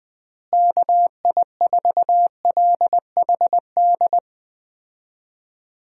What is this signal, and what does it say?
Morse code, heard as a keyed tone, about 20 wpm, 700 Hz. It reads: KI4LHD
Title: USA Call Signs - 6 Characters - Repeated 2x - 20wpm